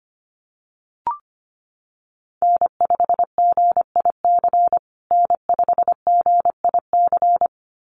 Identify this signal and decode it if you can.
Morse code, heard as a keyed tone, about 25 wpm, 700 Hz. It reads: N5GIC N5GIC
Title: USA Call Signs Repeated 2x 25wpm